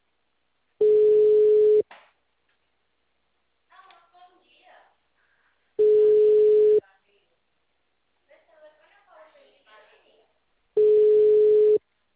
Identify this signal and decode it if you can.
Ringback tone (Europe, 425 Hz), with speech.